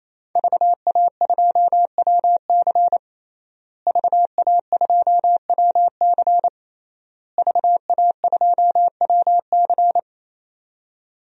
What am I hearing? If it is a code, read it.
Morse code, heard as a keyed tone, about 28 wpm, 700 Hz. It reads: VA2WC VA2WC VA2WC
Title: DXCC-100 Call Signs - 5 Characters - Repeated 3x - 28wpm